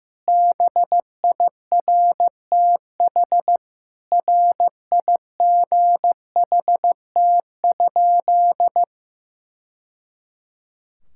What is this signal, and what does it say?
Morse code, heard as a keyed tone, about 15 wpm, 700 Hz. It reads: BIRTH RIGHT?
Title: Sentences from Top 1000 Words - Review Part 5 - 15wpm